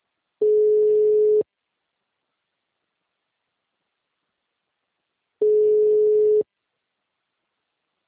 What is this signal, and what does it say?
Ringback tone (Europe, 425 Hz).